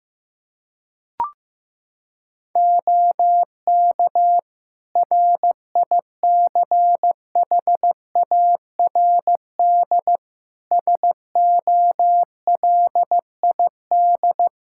Morse code, heard as a keyed tone, about 15 wpm, 700 Hz. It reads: OK RICHARD SOLID
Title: Practice QSO - Exchange 3 - More Info - 15wpm